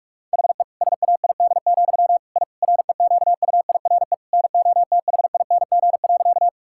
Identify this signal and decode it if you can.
Morse code, heard as a keyed tone, about 46 wpm, 700 Hz. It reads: HE SAID, I REQUIRE NOTHING.
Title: Sentences from Top 1000 Words - Review Part 6 - 45wpm